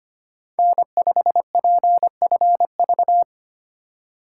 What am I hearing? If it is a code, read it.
Morse code, heard as a keyed tone, about 25 wpm, 700 Hz. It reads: N5PFV
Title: USA Call Signs - 5 Characters - 2x Character Spacing - 25wpm